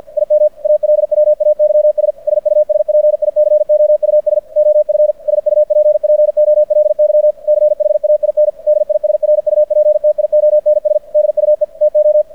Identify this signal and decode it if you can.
Morse code, heard as a keyed tone, about 50 wpm, 600 Hz. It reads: IM ALWAYS SUSPICOUS OF SUPPOLY CHAIN DISRUPTIONS DUE TO
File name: archived / CRSnet